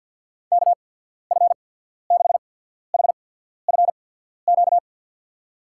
Morse code, heard as a keyed tone, about 53 wpm, spaced as for 14 wpm, 700 Hz. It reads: KF6HFQ
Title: USA Call Signs - 6 Characters - 8x Character Spacing - 50wpm